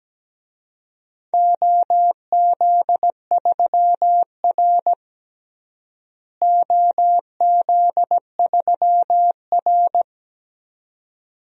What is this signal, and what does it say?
Morse code, heard as a keyed tone, about 17 wpm, 700 Hz. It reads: OZ3R OZ3R
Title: DXCC-100 Call Signs - 4 Characters - Repeated 2x - 17wpm